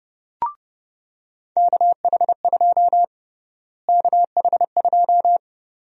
Morse code, heard as a keyed tone, about 30 wpm, 700 Hz. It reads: KH2 KH2
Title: USA Call Sign Prefixes - Repeated 2x - 30wpm